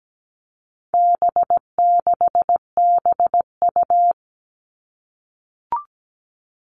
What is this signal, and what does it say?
Morse code, heard as a keyed tone, about 17 wpm, 700 Hz. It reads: B6BU
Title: All Cut Numbers - 4-Digit 17wpm